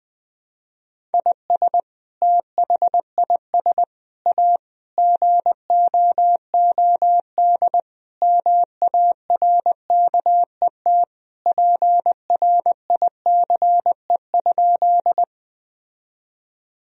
Morse code, heard as a keyed tone, about 20 wpm, 700 Hz. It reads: IS THIS A GOOD MARKET PRICE?
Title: Sentences from Top 1000 Words - Review Part 3 - 20wpm